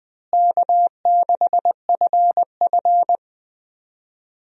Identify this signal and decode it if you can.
Morse code, heard as a keyed tone, about 20 wpm, 700 Hz. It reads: K6FF
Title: USA Call Signs - 4 Characters - Repeated 2x - 20wpm